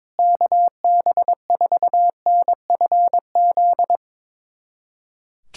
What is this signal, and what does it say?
Morse code, heard as a keyed tone, about 22 wpm, 700 Hz. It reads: KB4NFZ
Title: USA Callsign DE Callsign 22wpm